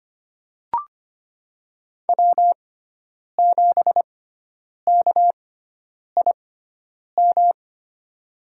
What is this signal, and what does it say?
Morse code, heard as a keyed tone, about 25 wpm, spaced as for 9 wpm, 700 Hz. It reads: W7KIM
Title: USA Call Signs - 5 Characters - 6x Character Spacing - 25wpm